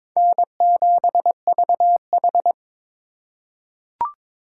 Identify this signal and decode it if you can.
Morse code, heard as a keyed tone, about 22 wpm, 700 Hz. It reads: N7VH
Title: USA Call Signs - 3x Character Spacing - 22wpm